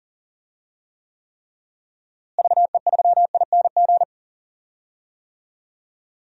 Morse code, heard as a keyed tone, about 39 wpm, 700 Hz. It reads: VE3ING
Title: DXCC-100 Call Signs - 6 Characters - Repeated 3x - 40wpm